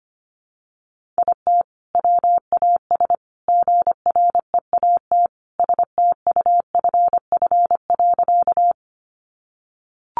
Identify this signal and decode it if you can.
Morse code, heard as a keyed tone, about 25 wpm, 700 Hz. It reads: IT WAS GREAT STUFF.
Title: Sentences from Top 1400 Words 25wpm